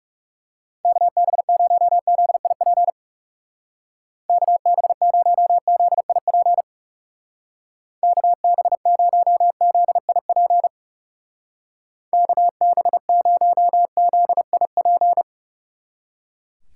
Morse code, heard as a keyed tone, about 38 wpm, 700 Hz. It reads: KB0ZIP KB0ZIP KB0ZIP KB0ZIP